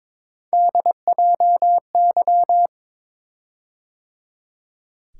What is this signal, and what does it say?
Morse code, heard as a keyed tone, about 22 wpm, 700 Hz. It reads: DJY
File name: practice / 22wpm / Three Letter-Number Combinations - 22wpm